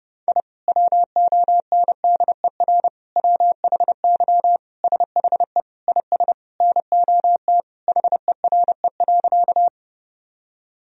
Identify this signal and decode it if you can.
Morse code, heard as a keyed tone, about 32 wpm, 700 Hz. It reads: I WONDER WHY SHE IS NOT HERE.
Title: Sentences from Top 1000 Words - Review Part 3 - 30wpm